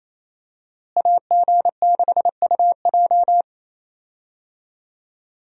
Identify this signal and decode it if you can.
Morse code, heard as a keyed tone, about 28 wpm, 700 Hz. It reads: AG6UJ
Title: USA Call Signs - 5 Characters - Repeated 2x - 28wpm